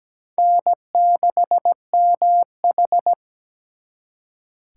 Morse code, heard as a keyed tone, about 20 wpm, 700 Hz. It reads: N6MH
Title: Speed Racing - USA 1x2 Call Signs 17wpm